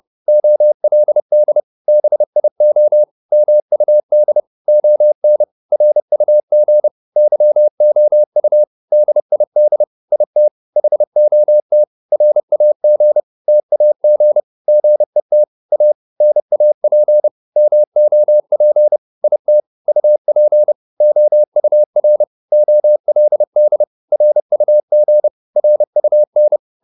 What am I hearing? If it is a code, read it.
Morse code, heard as a keyed tone, about 30 wpm, 600 Hz. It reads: OLD BIO MUD ON RUG YOU DID IT HOT RAG TAG GET A NAP MOP IT UP OUR OLD RUG RUN